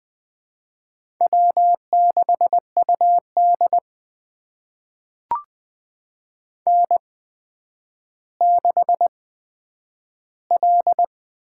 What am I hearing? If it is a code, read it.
Morse code, heard as a keyed tone, about 20 wpm, spaced as for 10 wpm, 700 Hz. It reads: W6UD N6L
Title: USA Call Signs - 4 Characters - 8x Character Spacing - 20wpm